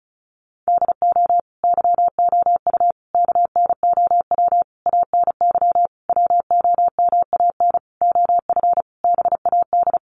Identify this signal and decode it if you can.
Morse code, heard as a keyed tone, about 34 wpm, 700 Hz. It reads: DO YOU KNOW ANY WOMAN OF BAD
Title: Sentences from Top 1200 Words 35wpm